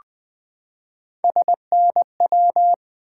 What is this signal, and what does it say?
Morse code, heard as a keyed tone, about 20 wpm, 700 Hz. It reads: SNW
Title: Three Letter-Number Combinations - 20wpm